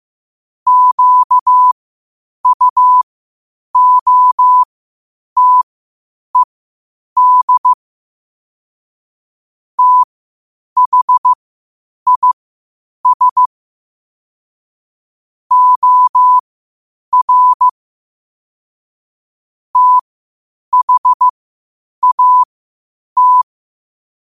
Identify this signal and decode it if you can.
Morse code, heard as a keyed tone, about 15 wpm, spaced as for 8 wpm, 1000 Hz. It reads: QUOTED THIS OR THAT
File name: qotdfiles / 8wpm